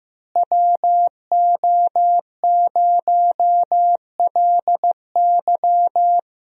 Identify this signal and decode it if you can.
Morse code, heard as a keyed tone, about 15 wpm, 700 Hz. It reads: WO0LY
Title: USA Call Signs - 5 Characters - Repeated 3x - 15wpm